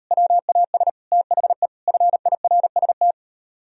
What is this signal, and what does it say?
Morse code, heard as a keyed tone, about 38 wpm, 700 Hz. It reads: WAS THE FIRST